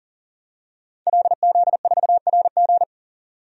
Morse code, heard as a keyed tone, about 40 wpm, 700 Hz. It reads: LZ4RG
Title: DXCC-100 Call Signs - 5 Characters - 40wpm